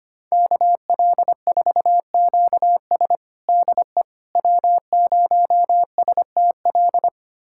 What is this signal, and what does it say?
Morse code, heard as a keyed tone, about 25 wpm, 700 Hz. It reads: KL4QS DE W0STL
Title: USA Callsign DE Callsign - 5 Characters - 25wpm